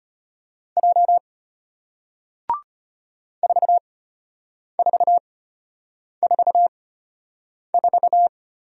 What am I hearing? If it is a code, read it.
Morse code, heard as a keyed tone, about 33 wpm, spaced as for 8 wpm, 700 Hz. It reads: J 4444